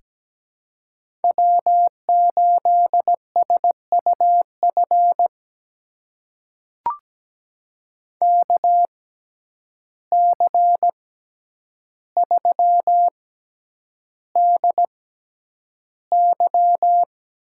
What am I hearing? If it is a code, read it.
Morse code, heard as a keyed tone, about 17 wpm, spaced as for 9 wpm, 700 Hz. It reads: W8SUF KC3DY
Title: USA Call Signs - 6x Character Spacing - 17wpm